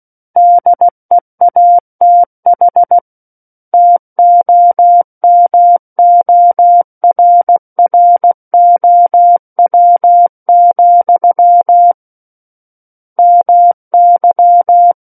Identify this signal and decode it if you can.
Morse code, heard as a keyed tone, about 16 wpm, 700 Hz. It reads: DEATH TOMORROW, MY
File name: Book One / 07-Chapter 7 - 16 WPM